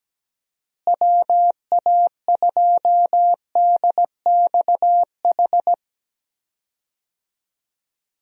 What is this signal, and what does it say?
Morse code, heard as a keyed tone, about 17 wpm, 700 Hz. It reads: WA2DXH
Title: USA Call Signs - 6 Characters - Repeated 2x - 17wpm